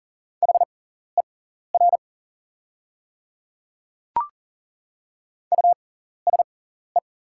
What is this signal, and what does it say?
Morse code, heard as a keyed tone, about 40 wpm, spaced as for 14 wpm, 700 Hz. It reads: HER USE
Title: 6x Character Spacing - Top 100 Words - 40wpm